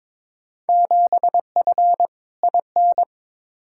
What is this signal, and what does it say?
Morse code, heard as a keyed tone, about 22 wpm, 700 Hz. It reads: 7F IN
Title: Field Day Exchanges Encore - 22wpm